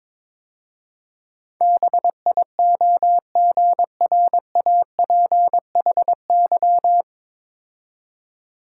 Morse code, heard as a keyed tone, about 22 wpm, 700 Hz. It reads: BIOGRAPHY
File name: 9 Letter Words - First 3 Letters Given - 22wpm